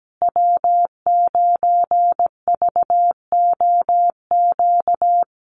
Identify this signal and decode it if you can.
Morse code, heard as a keyed tone, about 17 wpm, 700 Hz. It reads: W9VOQ
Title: USA Call Signs Repeated 3x 17wpm